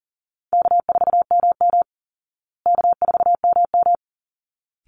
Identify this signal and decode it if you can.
Morse code, heard as a keyed tone, about 40 wpm, 700 Hz. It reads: K4MM K4MM
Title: Easy Call Signs Repeated 2x - 40wpm